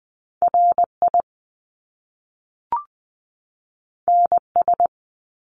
Morse code, heard as a keyed tone, about 20 wpm, 700 Hz. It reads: RI NS
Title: Top Two Letter Combinations in Words 20wpm